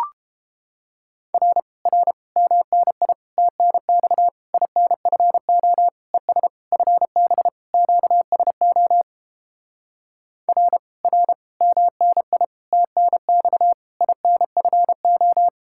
Morse code, heard as a keyed tone, about 29 wpm, 700 Hz. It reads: R R MNI TNX INFO ES FB QSO R R MNI TNX INFO